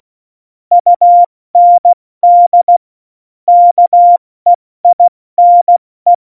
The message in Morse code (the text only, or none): UND KEINE